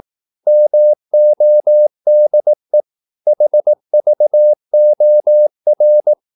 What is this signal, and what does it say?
Morse code, heard as a keyed tone, about 18 wpm, 600 Hz. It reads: MODE HVOR
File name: Korte ord DK 18wpm